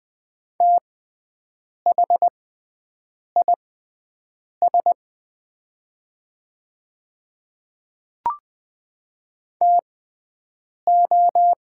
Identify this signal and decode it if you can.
Morse code, heard as a keyed tone, about 20 wpm, spaced as for 7 wpm, 700 Hz. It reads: THIS TO
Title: 6x Character Spacing - Top 100 Words - 20wpm